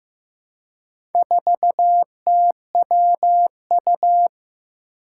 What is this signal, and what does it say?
Morse code, heard as a keyed tone, about 15 wpm, 700 Hz. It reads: 4TWU
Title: Four Challenging Letter-Number Combinations 15wpm